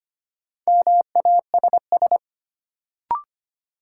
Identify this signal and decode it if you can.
Morse code, heard as a keyed tone, about 25 wpm, 700 Hz. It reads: MASS
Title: Sets of 5 Words Encore - New Word Drill - 25wpm